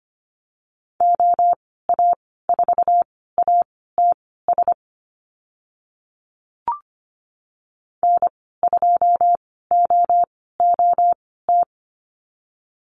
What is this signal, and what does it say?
Morse code, heard as a keyed tone, about 25 wpm, spaced as for 16 wpm, 690 Hz. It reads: OA4ATS N2OOT
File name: Lesson 15g - Number 5 - Call Signs 25wpm